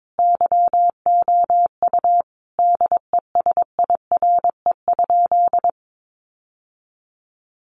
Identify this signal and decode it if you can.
Morse code, heard as a keyed tone, about 22 wpm, 700 Hz. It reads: YOU DESIRE?
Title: Sentences from Top 1100 Words 22wpm